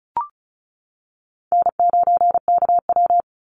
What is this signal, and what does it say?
Morse code, heard as a keyed tone, about 34 wpm, 700 Hz. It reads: N9KW
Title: NA SOTA Chaser Callsigns 35wpm